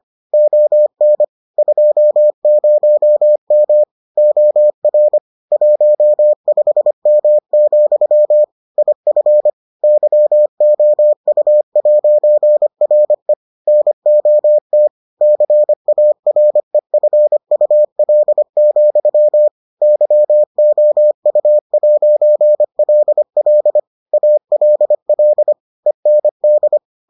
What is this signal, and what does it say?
Morse code, heard as a keyed tone, about 25 wpm, 600 Hz. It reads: ON 20M OR 15M, IF YOU'RE NOT CAREFUL, YOU'LL ALL END